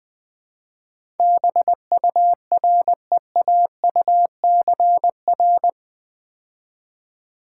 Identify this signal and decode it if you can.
Morse code, heard as a keyed tone, about 20 wpm, 700 Hz. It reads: BUREAUCR
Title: Build A Word - 4 Letters At A Time - 11-Letter Words - 20wpm